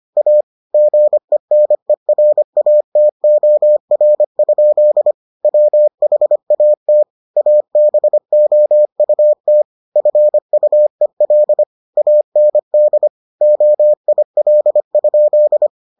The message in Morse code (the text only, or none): A GENERATOR? WHAT ABOUT FUEL AND OIL?